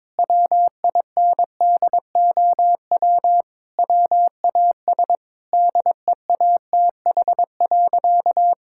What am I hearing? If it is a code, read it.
Morse code, heard as a keyed tone, about 22 wpm, 700 Hz. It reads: WINDOW WAS DEATH.
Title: Sentences from Top 1600 Words 22wpm